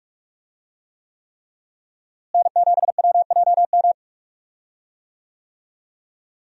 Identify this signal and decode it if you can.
Morse code, heard as a keyed tone, about 46 wpm, 700 Hz. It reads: N7WJM
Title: USA Call Signs - 5 Characters - Repeated 3x - 45wpm